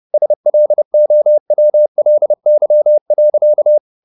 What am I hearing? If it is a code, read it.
Morse code, heard as a keyed tone, about 30 wpm, 600 Hz. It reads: SLOWLY.
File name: the old man and the sea / the old man and the sea 1 30wpm